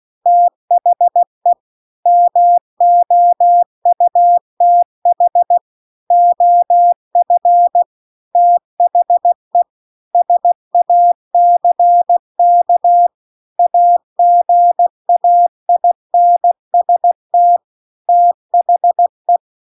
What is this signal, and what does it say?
Morse code, heard as a keyed tone, about 16 wpm, 700 Hz. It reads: THE MOUTH OF THE SACK AGAINST THE
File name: CW-The-Big-Twoheaded-River-2-16WPM